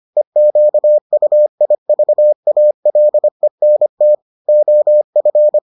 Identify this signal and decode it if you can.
Morse code, heard as a keyed tone, about 25 wpm, 600 Hz. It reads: EQUIVALENT OF